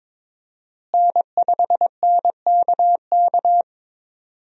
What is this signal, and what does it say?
Morse code, heard as a keyed tone, about 22 wpm, 700 Hz. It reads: N5NKK
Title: USA Call Signs - 5 Characters - Repeated 2x - 22wpm